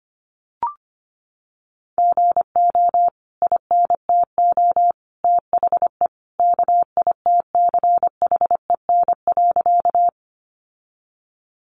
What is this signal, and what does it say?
Morse code, heard as a keyed tone, about 25 wpm, 700 Hz. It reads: GO INTO THE KITCHEN.
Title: Sentences from Top 1400 Words 25wpm